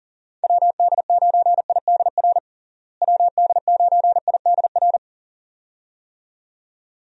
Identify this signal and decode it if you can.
Morse code, heard as a keyed tone, about 37 wpm, 700 Hz. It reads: WD9IDR WD9IDR
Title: USA Call Signs - 6 Characters - Repeated 2x - 40wpm